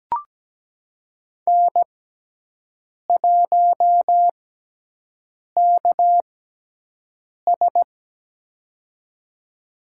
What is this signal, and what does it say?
Morse code, heard as a keyed tone, about 17 wpm, spaced as for 6 wpm, 700 Hz. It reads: N1KS
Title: USA Call Signs - 4 Characters - 6x Character Spacing - 17wpm